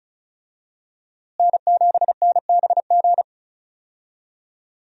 Morse code, heard as a keyed tone, about 36 wpm, 700 Hz. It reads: N7NBG
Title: USA Call Signs - 5 Characters - Repeated 2x - 35wpm